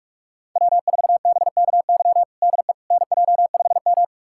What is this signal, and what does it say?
Morse code, heard as a keyed tone, about 46 wpm, 700 Hz. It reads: W4BKY DE NJ5M
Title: USA Callsign DE Callsign 45wpm